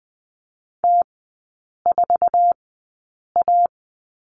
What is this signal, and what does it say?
Morse code, heard as a keyed tone, about 20 wpm, spaced as for 8 wpm, 700 Hz. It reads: T4A
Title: 047 Lesson 9e - Number 4 - Three Characters 20wpm